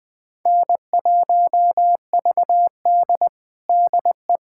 Morse code, heard as a keyed tone, about 20 wpm, 700 Hz. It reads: N1VD DE
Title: USA Callsign DE Callsign - 4 Characters - 20wpm